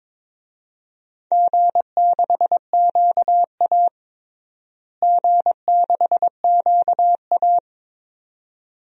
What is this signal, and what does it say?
Morse code, heard as a keyed tone, about 22 wpm, 700 Hz. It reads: G6QA G6QA
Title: DXCC-100 Call Signs - 4 Characters - Repeated 2x - 22wpm